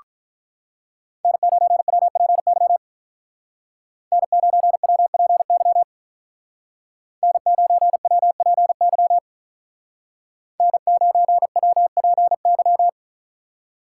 Morse code, heard as a keyed tone, about 47 wpm, 700 Hz. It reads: N9WPY N9WPY N9WPY N9WPY